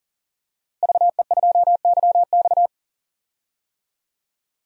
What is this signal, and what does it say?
Morse code, heard as a keyed tone, about 40 wpm, 700 Hz. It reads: VE2YX
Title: DXCC-100 Call Signs - 5 Characters - Repeated 2x - 40wpm